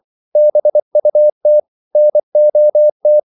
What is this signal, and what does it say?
Morse code, heard as a keyed tone, about 24 wpm, 600 Hz. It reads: BUT NOT